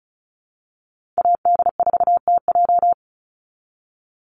Morse code, heard as a keyed tone, about 34 wpm, 700 Hz. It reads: AD4TJ
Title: Call Signs - 35wpm